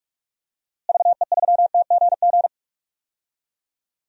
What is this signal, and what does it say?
Morse code, heard as a keyed tone, about 42 wpm, 700 Hz. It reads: VE3TGG
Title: DXCC-100 Call Signs - 6 Characters - Repeated 3x - 45wpm